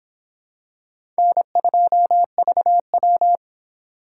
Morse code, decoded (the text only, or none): N2VW